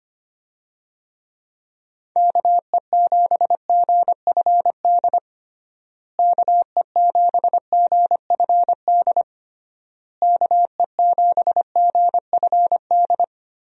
Morse code, heard as a keyed tone, about 25 wpm, 700 Hz. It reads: KE7GFD KE7GFD KE7GFD
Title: USA Call Signs - 6 Characters - Repeated 3x - 25wpm